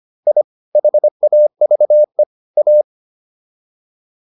Morse code, heard as a keyed tone, about 25 wpm, 600 Hz. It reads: I HAVE A